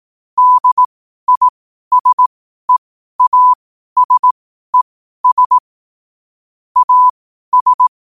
Morse code, heard as a keyed tone, about 18 wpm, spaced as for 12 wpm, 1000 Hz. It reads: DISEASES AS
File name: qotdfiles / 12wpm